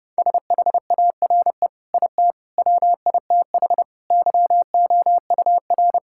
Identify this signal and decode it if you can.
Morse code, heard as a keyed tone, about 32 wpm, 700 Hz. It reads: SHARE IT WITH YOUR
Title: Sentences from Top 1000 Words - Review Part 6 - 30wpm